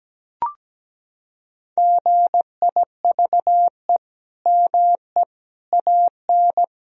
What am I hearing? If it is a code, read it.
Morse code, heard as a keyed tone, about 17 wpm, 700 Hz. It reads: GIVE ME AN
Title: Sentences from Top 1000 Words - Review Part 1 - 17wpm